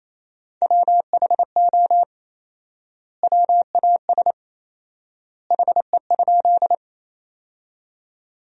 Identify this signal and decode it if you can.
Morse code, heard as a keyed tone, about 28 wpm, 700 Hz. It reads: WHO WAS HE?
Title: 4x Word-Spacing - Sentences from Top 100 Words - 28wpm